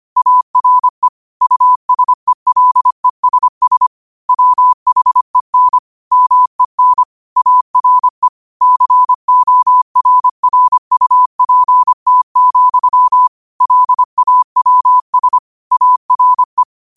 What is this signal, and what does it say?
Morse code, heard as a keyed tone, about 25 wpm, 1000 Hz. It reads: ARE USELESS WHEN MEN ARE CORRUPT, LAWS ARE
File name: qotdfiles / 25wpm